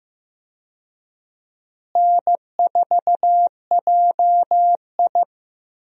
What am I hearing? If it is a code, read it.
Morse code, heard as a keyed tone, about 15 wpm, 700 Hz. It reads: N4JI
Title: USA Call Signs - 4 Characters - Repeated 2x - 15wpm